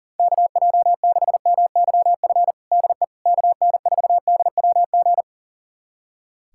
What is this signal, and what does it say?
Morse code, heard as a keyed tone, about 37 wpm, 700 Hz. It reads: KJ6MYF DE KN4DWG
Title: USA Callsign DE Callsign 40wpm